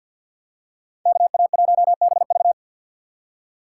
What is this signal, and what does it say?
Morse code, heard as a keyed tone, about 51 wpm, 700 Hz. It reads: KA1BV
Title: USA Call Signs - 5 Characters - 2x Character Spacing - 50wpm